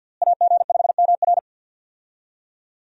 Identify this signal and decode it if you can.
Morse code, heard as a keyed tone, about 53 wpm, 700 Hz. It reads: AG5MR
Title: USA Call Signs - 5 Characters - Repeated 3x - 50wpm